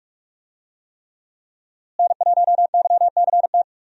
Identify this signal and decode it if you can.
Morse code, heard as a keyed tone, about 46 wpm, 700 Hz. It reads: N1YCT
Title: USA Call Signs - 5 Characters - Repeated 3x - 45wpm